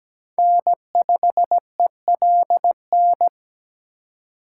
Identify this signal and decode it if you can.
Morse code, heard as a keyed tone, about 20 wpm, 700 Hz. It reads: N5ELN
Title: Speed Racing - USA 1x3 Call Signs 17wpm